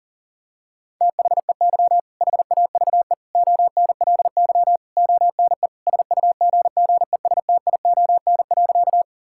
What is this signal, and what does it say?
Morse code, heard as a keyed tone, about 40 wpm, 700 Hz. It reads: THEY HAVE ONLY ONE SUGGESTION.
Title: Sentences from Top 1600 Words 40wpm